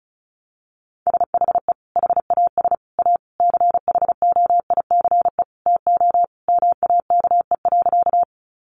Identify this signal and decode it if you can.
Morse code, heard as a keyed tone, about 34 wpm, 700 Hz. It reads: SHE HAS A CHOICE TO MAKE.
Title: Sentences from Top 900 Words 35wpm